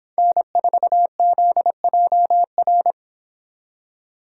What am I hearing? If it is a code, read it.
Morse code, heard as a keyed tone, about 24 wpm, 700 Hz. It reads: N4ZJR